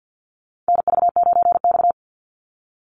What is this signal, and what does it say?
Morse code, heard as a keyed tone, about 50 wpm, 700 Hz. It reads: NV9X
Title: Call Signs Repeated 2x - 50wpm